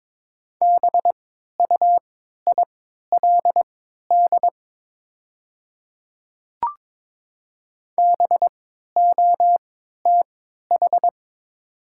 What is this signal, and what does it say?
Morse code, heard as a keyed tone, about 22 wpm, spaced as for 12 wpm, 690 Hz. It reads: BUILD BOTH
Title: 3x Character Spacing - Top 300 Words - 22wpm